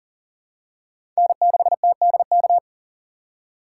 Morse code, heard as a keyed tone, about 40 wpm, 700 Hz. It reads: N6TDK
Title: USA Call Signs - 4x Character Spacing - 40wpm